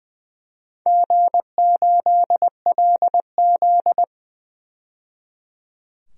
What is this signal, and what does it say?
Morse code, heard as a keyed tone, about 20 wpm, 700 Hz. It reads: G8LZ
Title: DXCC-100 Call Signs - 4 Characters - 20wpm